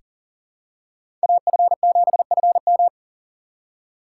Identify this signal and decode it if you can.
Morse code, heard as a keyed tone, about 40 wpm, 700 Hz. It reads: AF7FM
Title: USA Call Signs - 5 Characters - 3x Character Spacing - 40wpm